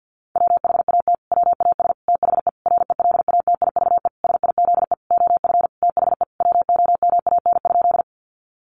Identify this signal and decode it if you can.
Morse code, heard as a keyed tone, about 50 wpm, 700 Hz. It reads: WHAT WAS THE RELATIVE SIZE OF THE WOMAN?
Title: Sentences from Top 1400 Words 50wpm